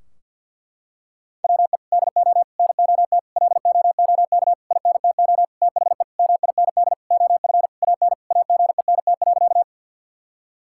Morse code, heard as a keyed tone, about 50 wpm, 700 Hz. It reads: WE DO NOT LOOK INTO THE MIND OF AN AGENT.
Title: Sentences from Top 1600 Words 50wpm